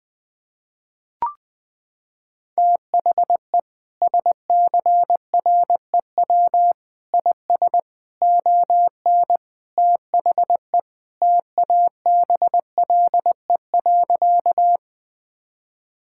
Morse code, heard as a keyed tone, about 20 wpm, 700 Hz. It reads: THE SCREW IS ON THE TABLE.
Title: Sentences from Top 1600 Words 20wpm